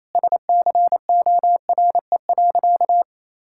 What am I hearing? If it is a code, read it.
Morse code, heard as a keyed tone, about 28 wpm, 700 Hz. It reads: SCORE.
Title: Sentences from Top 1600 Words 28wpm